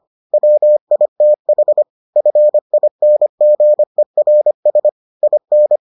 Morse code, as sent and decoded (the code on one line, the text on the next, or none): .-- .. - .... / ..-. .. -. --. .-. ... / .. -.
WITH FINGERS IN